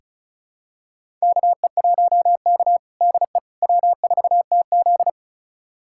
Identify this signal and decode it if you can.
Morse code, heard as a keyed tone, about 36 wpm, 700 Hz. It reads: KE1K DE W4TZ
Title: USA Callsign DE Callsign - 4 Characters - 35wpm